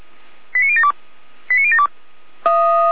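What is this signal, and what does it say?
Five-tone selective call (ZVEI), heard as quick runs of five tones, tones of about 80 ms. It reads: 89072 89072
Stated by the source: ZVEI five-tone selective call 89080, received by SDR.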